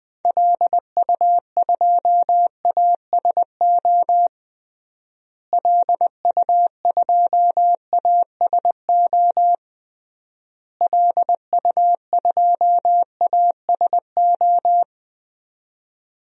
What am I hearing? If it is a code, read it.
Morse code, heard as a keyed tone, about 21 wpm, 700 Hz. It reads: LU2ASO LU2ASO LU2ASO
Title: DXCC-100 Call Signs - 6 Characters - Repeated 3x - 20wpm